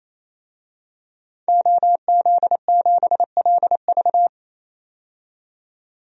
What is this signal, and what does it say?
Morse code, heard as a keyed tone, about 28 wpm, 700 Hz. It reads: OZ7LV
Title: DXCC-100 Call Signs - 5 Characters - 28wpm